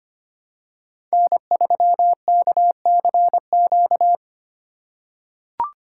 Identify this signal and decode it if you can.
Morse code, heard as a keyed tone, about 25 wpm, 700 Hz. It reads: N3KCQ
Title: USA Call Signs - 5 Characters - 2x Character Spacing - 25wpm